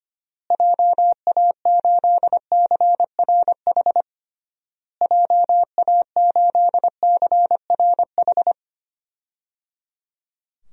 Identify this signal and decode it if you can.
Morse code, heard as a keyed tone, about 25 wpm, 700 Hz. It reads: JA8CRH JA8CRH
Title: DXCC-100 Call Signs - 6 Characters - Repeated 3x - 25wpm